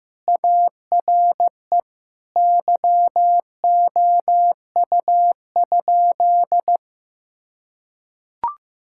Morse code, heard as a keyed tone, about 15 wpm, 700 Hz. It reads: ARE YOU?
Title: Sentences from Top 1000 Words - Review Part 6 - 15wpm